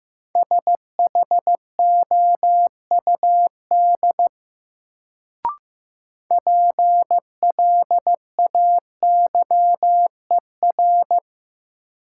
Morse code, heard as a keyed tone, about 15 wpm, 700 Hz. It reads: SHOUD PLAYER
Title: Sentences from Top 1000 Words - Review Part 4 - New Word Drill - 15wpm